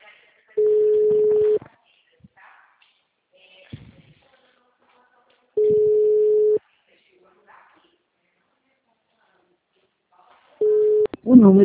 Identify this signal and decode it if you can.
Ringback tone (Europe, 425 Hz), with speech.